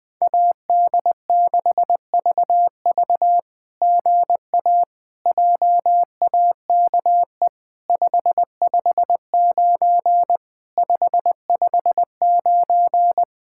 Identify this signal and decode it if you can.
Morse code, heard as a keyed tone, about 20 wpm, 700 Hz. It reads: AD6VV GA JAKE 559 559
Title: SOTA QSO Practice 20wpm